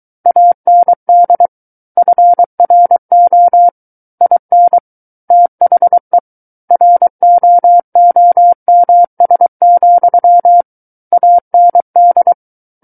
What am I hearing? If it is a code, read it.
Morse code, heard as a keyed tone, about 23 wpm, 700 Hz. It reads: AND FRO IN THE ROOMS, AND
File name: Book One / 14-Chapter 14 - 23 WPM